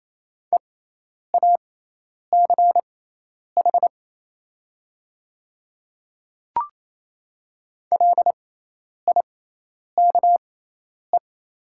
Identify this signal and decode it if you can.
Morse code, heard as a keyed tone, about 28 wpm, spaced as for 10 wpm, 700 Hz. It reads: EACH LIKE
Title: 6x Character Spacing - Top 100 Words - 28wpm